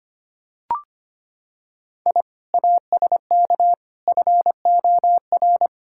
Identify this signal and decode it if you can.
Morse code, heard as a keyed tone, about 25 wpm, 700 Hz. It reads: I ASK FOR
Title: Sentences from Top 1000 Words - Review Part 5 - 25wpm